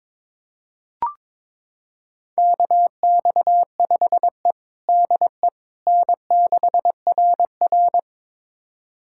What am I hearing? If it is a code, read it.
Morse code, heard as a keyed tone, about 22 wpm, 700 Hz. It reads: KX5E DE N6RR
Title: USA Callsign DE Callsign - 4 Characters - 22wpm